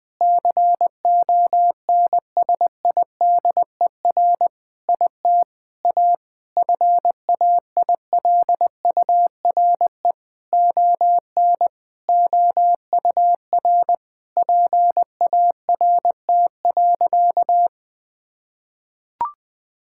Morse code, heard as a keyed tone, about 20 wpm, 700 Hz. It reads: CONSIDER IT A FAILURE ON OUR PART.
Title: Sentences from Top 1000 Words - Review Part 6 - 20wpm